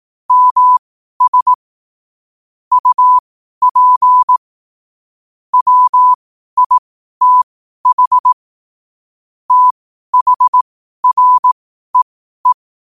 Morse code, heard as a keyed tone, about 18 wpm, spaced as for 12 wpm, 1000 Hz. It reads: MS UP WITH THREE
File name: qotdfiles / 12wpm